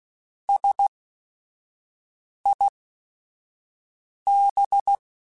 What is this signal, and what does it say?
Morse code, heard as a keyed tone, about 16 wpm, spaced as for 5 wpm, 800 Hz. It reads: SIB
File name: Technical Subjects / Technical Subjects-mp3s / AND Gate @5